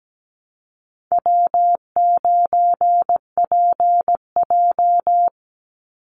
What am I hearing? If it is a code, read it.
Morse code, heard as a keyed tone, about 17 wpm, 700 Hz. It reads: W9PJ
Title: Call Signs Repeated 3x - 17wpm